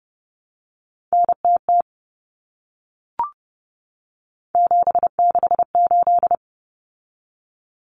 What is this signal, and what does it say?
Morse code, heard as a keyed tone, about 30 wpm, 700 Hz. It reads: NTT 768
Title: Common Cut Numbers - 3-Digit 30wpm